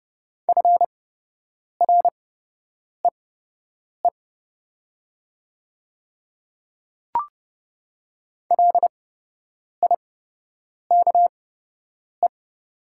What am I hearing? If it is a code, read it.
Morse code, heard as a keyed tone, about 30 wpm, spaced as for 8 wpm, 700 Hz. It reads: FREE LIKE
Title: 8x Character Spacing - Top 500 Words - 30wpm